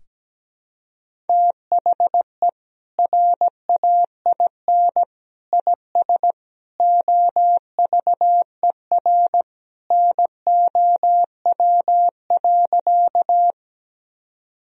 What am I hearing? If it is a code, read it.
Morse code, heard as a keyed tone, about 17 wpm, 700 Hz. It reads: THE RAIN IS OVER NOW.
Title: Sentences from Top 1000 Words - Review Part 1 - 17wpm